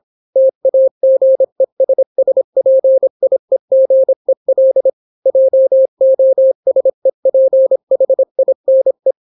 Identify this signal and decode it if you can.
Morse code, heard as a keyed tone, about 26 wpm, 530 Hz. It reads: TAGESSPIEGEL JOSEPHINE